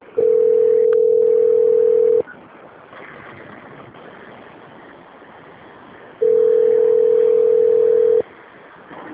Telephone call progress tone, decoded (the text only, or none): ringback tone